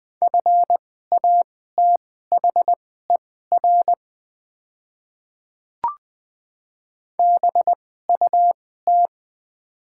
Morse code, heard as a keyed tone, about 20 wpm, spaced as for 14 wpm, 700 Hz. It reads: FATHER BUT
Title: 2x Character Spacing - Top 200 Words - 20wpm